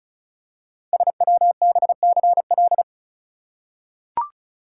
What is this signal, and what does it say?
Morse code, heard as a keyed tone, about 36 wpm, 690 Hz. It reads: SWBCL
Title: Code Groups - 5 Letters - 35wpm